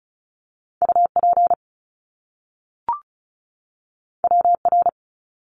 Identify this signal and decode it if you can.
Morse code, heard as a keyed tone, about 36 wpm, 700 Hz. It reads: UP WR
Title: Top Two Letter Combinations in Words 35wpm